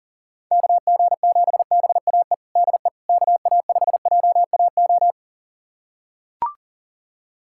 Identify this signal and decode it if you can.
Morse code, heard as a keyed tone, about 40 wpm, 700 Hz. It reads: KG7BAE DE KA5JAO
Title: USA Callsign DE Callsign - 6 Characters - 40wpm